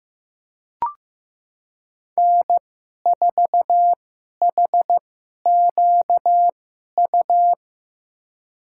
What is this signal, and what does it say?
Morse code, heard as a keyed tone, about 15 wpm, spaced as for 11 wpm, 700 Hz. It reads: N4HQU
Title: USA Call Signs - 5 Characters - 2x Character Spacing - 15wpm